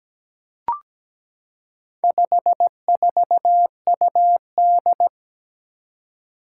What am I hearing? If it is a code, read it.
Morse code, heard as a keyed tone, about 17 wpm, 700 Hz. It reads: 54UD
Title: Four Challenging Letter-Number Combinations 17wpm